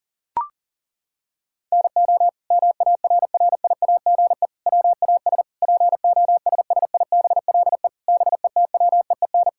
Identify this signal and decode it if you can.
Morse code, heard as a keyed tone, about 40 wpm, 700 Hz. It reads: NO MARRIAGE WAS POSSIBLE BETWEEN
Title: Sentences from Top 1000 Words - Review Part 5 - 40wpm